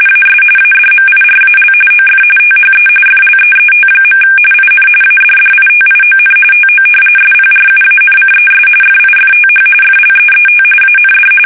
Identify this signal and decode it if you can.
RTTY, about 45.45 baud. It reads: AS ADMIRAL OF THE FLEET. IN ADDITION TO SPRUANCES FIFTH FLEET AND
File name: fsk850 / battleshipnj